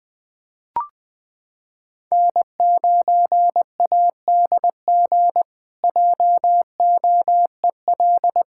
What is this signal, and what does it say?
Morse code, heard as a keyed tone, about 20 wpm, 700 Hz. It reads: N9ADG JOEL
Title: K1USN SST QSO Practice - 20wpm